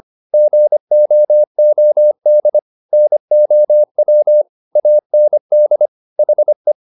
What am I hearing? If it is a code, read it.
Morse code, heard as a keyed tone, about 25 wpm, 600 Hz. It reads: GOOD NOW AND HE